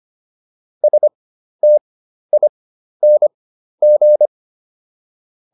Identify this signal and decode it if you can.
Morse code, heard as a keyed tone, about 25 wpm, spaced as for 12 wpm, 600 Hz. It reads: STING